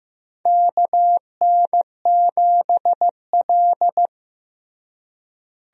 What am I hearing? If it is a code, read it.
Morse code, heard as a keyed tone, about 15 wpm, 700 Hz. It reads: KN7L
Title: USA Call Signs - 4 Characters - Repeated 3x - 15wpm